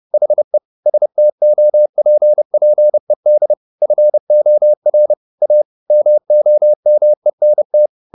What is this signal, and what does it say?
Morse code, heard as a keyed tone, about 30 wpm, 600 Hz. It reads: HE STOPPED FOR A MOMENT